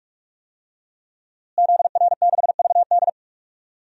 Morse code, heard as a keyed tone, about 42 wpm, 700 Hz. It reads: ZR6VD
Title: DXCC-100 Call Signs - 5 Characters - Repeated 2x - 45wpm